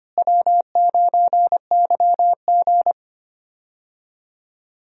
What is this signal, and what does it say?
Morse code, heard as a keyed tone, about 25 wpm, 700 Hz. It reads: W9YG
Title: USA Call Signs - 4 Characters - Repeated 2x - 25wpm